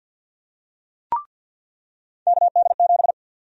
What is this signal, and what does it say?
Morse code, heard as a keyed tone, about 48 wpm, 700 Hz. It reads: KD7
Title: USA Call Sign Prefixes - 50wpm